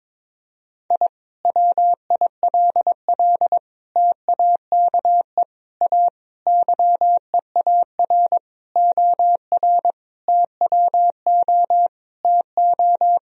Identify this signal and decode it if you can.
Morse code, heard as a keyed tone, about 22 wpm, 700 Hz. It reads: I WILL TAKE A YEAR OR TWO TO
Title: Sentences from Top 1000 Words - Review Part 1 - 22wpm